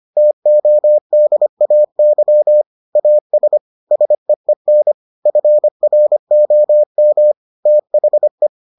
Morse code, heard as a keyed tone, about 25 wpm, 600 Hz. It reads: TODAY AS SEEN FROM THE